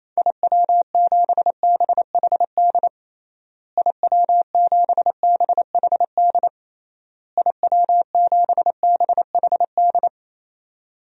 Morse code, heard as a keyed tone, about 28 wpm, 700 Hz. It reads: IW7BHD IW7BHD IW7BHD
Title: DXCC-100 Call Signs - 6 Characters - Repeated 3x - 28wpm